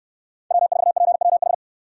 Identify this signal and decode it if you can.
Morse code, heard as a keyed tone, about 67 wpm, 700 Hz. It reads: W4PWH